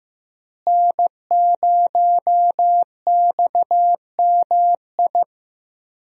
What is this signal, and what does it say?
Morse code, heard as a keyed tone, about 15 wpm, 700 Hz. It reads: N0XMI
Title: USA Call Signs - 5 Characters - Repeated 2x - 15wpm